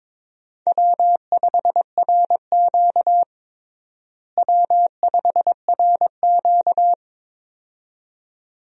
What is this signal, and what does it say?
Morse code, heard as a keyed tone, about 22 wpm, 700 Hz. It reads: W5RQ W5RQ
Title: USA Call Signs - 4 Characters - Repeated 2x - 22wpm